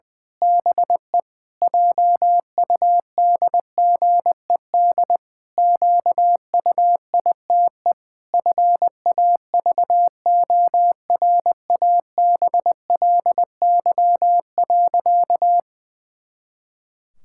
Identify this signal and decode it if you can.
Morse code, heard as a keyed tone, about 20 wpm, 700 Hz. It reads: BE JUDGED QUITE FAVORABLY.
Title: Pangram Challenge v2 - 20wpm